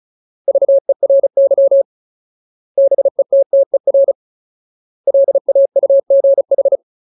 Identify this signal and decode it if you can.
Morse code, heard as a keyed tone, about 34 wpm, 550 Hz. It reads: VERY BETTER LAUGH
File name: Ultimate Word List / 35WPM Ultimate